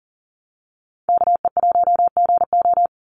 Morse code, heard as a keyed tone, about 42 wpm, 700 Hz. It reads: KE1GO
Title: USA Call Signs 40wpm